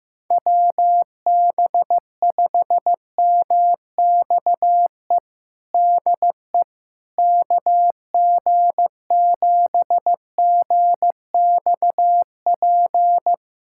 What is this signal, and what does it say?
Morse code, heard as a keyed tone, about 15 wpm, 700 Hz. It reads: WB5MXE DE KG7GXP
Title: USA Callsign DE Callsign - 6 Characters - 15wpm